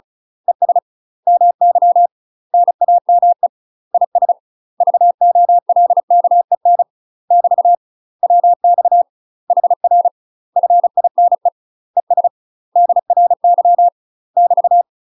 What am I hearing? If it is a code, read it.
Morse code, heard as a keyed tone, about 34 wpm, 700 Hz. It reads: ES MY NAME IS VOLKEN <BT> WX HR FINE ES DRY <BT>
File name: UK MASTER QSOs TRAINING / 35WPM / UK MASTER QSOs 0037